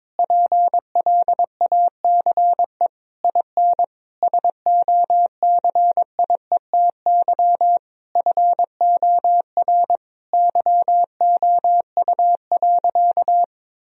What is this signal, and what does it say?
Morse code, heard as a keyed tone, about 22 wpm, 700 Hz. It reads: PLACE IN SOCIETY FOR YOU.
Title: Sentences from Top 1000 Words - Review Part 4 - 22wpm